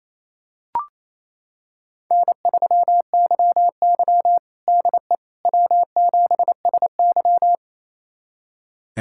Morse code, heard as a keyed tone, about 28 wpm, 700 Hz. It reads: N3YY DE W7SY
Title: USA Callsign DE Callsign - 4 Characters - 28wpm